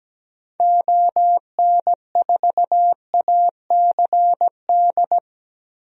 Morse code, heard as a keyed tone, about 17 wpm, 700 Hz. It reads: ON4ACD
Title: DXCC-100 Call Signs - 6 Characters - Repeated 2x - 17wpm